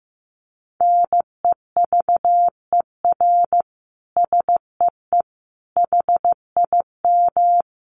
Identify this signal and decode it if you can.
Morse code, heard as a keyed tone, about 15 wpm, 700 Hz. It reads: NEVER SEE HIM
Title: Sentences from Top 300 Words 15wpm